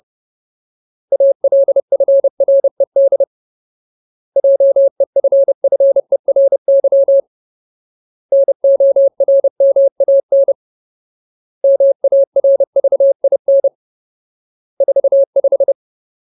Morse code, heard as a keyed tone, about 30 wpm, 550 Hz. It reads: ALFRED JEFFERY NORMAN MARVIN 45
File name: Ultimate Word List / 30WPM Ultimate